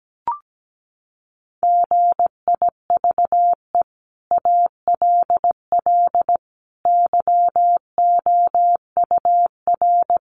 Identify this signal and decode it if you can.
Morse code, heard as a keyed tone, about 17 wpm, 700 Hz. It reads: GIVE ALL YOUR
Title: Sentences from Top 600 Words 17wpm